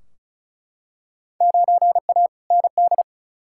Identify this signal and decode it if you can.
Morse code, heard as a keyed tone, about 36 wpm, 700 Hz. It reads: 9A ND
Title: Field Day Exchanges Encore - 35wpm